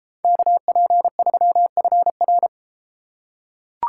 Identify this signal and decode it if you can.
Morse code, heard as a keyed tone, about 31 wpm, 700 Hz. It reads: KP3FR